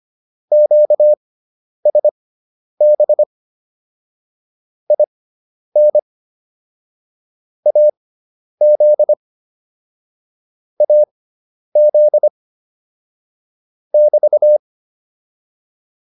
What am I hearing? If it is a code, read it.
Morse code, heard as a keyed tone, about 25 wpm, spaced as for 10 wpm, 600 Hz. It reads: QSB IN AZ AZ <BT>